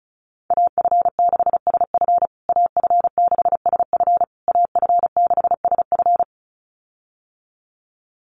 Morse code, heard as a keyed tone, about 34 wpm, 700 Hz. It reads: AF6SF AF6SF AF6SF
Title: USA Call Signs Repeated 3x 35wpm